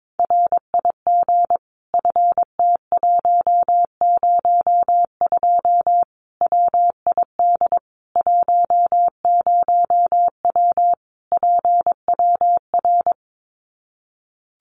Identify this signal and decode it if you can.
Morse code, heard as a keyed tone, about 22 wpm, 700 Hz. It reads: RIG FT102 WID 10W PWR
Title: Common QSO Phrases 22wpm